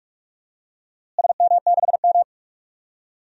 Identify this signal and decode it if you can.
Morse code, heard as a keyed tone, about 44 wpm, 700 Hz. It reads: SM6M
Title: DXCC-100 Call Signs - 4 Characters - Repeated 2x - 45wpm